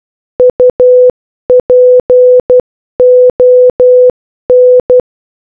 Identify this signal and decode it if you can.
Morse code, heard as a keyed tone, about 12 wpm, 500 Hz. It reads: UPON